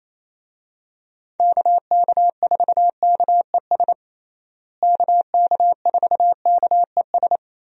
Morse code, heard as a keyed tone, about 28 wpm, 700 Hz. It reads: KK4KES KK4KES
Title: USA Call Signs - 6 Characters - Repeated 2x - 28wpm